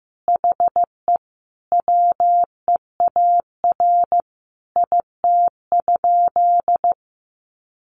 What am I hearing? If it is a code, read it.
Morse code, heard as a keyed tone, about 15 wpm, 700 Hz. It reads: HE WEAR IT?
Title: Sentences from Top 1500 Words 15wpm